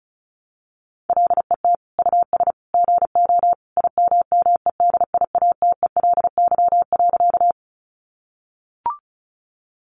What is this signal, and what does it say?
Morse code, heard as a keyed tone, about 34 wpm, 700 Hz. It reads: LET US GO IMMEDIATELY.
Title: Sentences from Top 1100 Words 35wpm